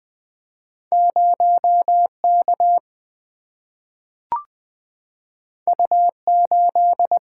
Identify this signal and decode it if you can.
Morse code, heard as a keyed tone, about 20 wpm, 700 Hz. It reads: 0 K U 8
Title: Single-Letter Single-Number 20wpm